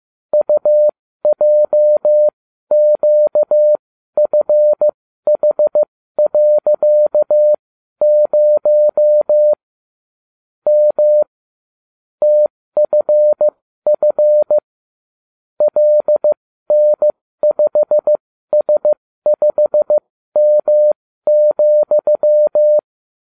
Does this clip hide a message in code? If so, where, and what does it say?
Yes: Morse code from 0.3 to 22.8 s, heard as a keyed tone, about 15 wpm, spaced as for 12 wpm, 600 Hz. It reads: UJQFH.0 M TFF LN5S5M,